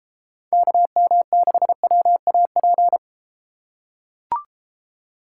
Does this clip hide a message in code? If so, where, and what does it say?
Yes: Morse code from 0.5 to 3.0 s, heard as a keyed tone, about 33 wpm, 700 Hz. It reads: KM6WAP